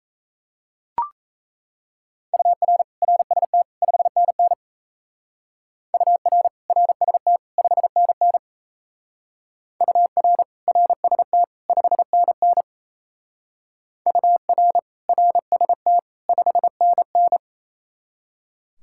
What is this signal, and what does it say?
Morse code, heard as a keyed tone, about 34 wpm, 700 Hz. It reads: UR RST 5NN UR RST 5NN UR RST 5NN UR RST 5NN
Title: Speed-Racing Common QSO Phrases Encore - 28wpm